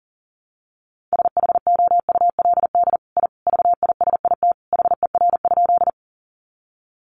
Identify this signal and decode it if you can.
Morse code, heard as a keyed tone, about 40 wpm, 700 Hz. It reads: SHOULD I VISIT HER?
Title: Sentences from Top 900 Words 40wpm